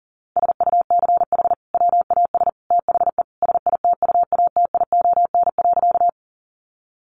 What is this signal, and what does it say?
Morse code, heard as a keyed tone, about 40 wpm, 700 Hz. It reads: SUCH WAS THE SITUATION.
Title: Sentences from Top 1200 Words 40wpm